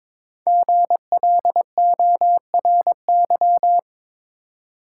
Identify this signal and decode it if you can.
Morse code, heard as a keyed tone, about 22 wpm, 700 Hz. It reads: GLORY
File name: 5 Letter Words - First 3 Letters Given - 22wpm